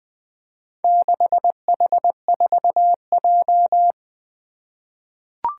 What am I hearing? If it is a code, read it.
Morse code, heard as a keyed tone, about 21 wpm, 700 Hz. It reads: 6H4J
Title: Four Challenging Letter-Number Combinations 20wpm